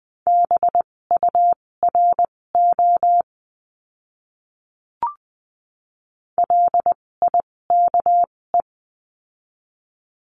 Morse code, heard as a keyed tone, about 20 wpm, spaced as for 16 wpm, 700 Hz. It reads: BURO LIKE
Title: 256 Lesson 42a - Words 12-20wpm